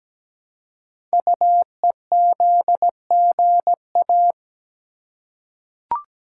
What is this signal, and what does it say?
Morse code, heard as a keyed tone, about 17 wpm, 700 Hz. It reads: UEZGA
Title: Code Groups - 5 Letter:Numbers - 17wpm